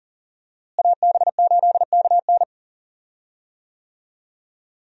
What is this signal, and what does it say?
Morse code, heard as a keyed tone, about 40 wpm, 700 Hz. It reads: AB8KN
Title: USA Call Signs - 5 Characters - Repeated 2x - 40wpm